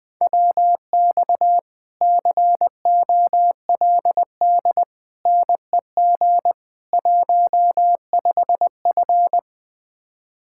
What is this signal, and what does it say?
Morse code, heard as a keyed tone, about 20 wpm, 700 Hz. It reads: WX COLD NEG 15F
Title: Speed-Racing Common QSO Phrases Encore - 17wpm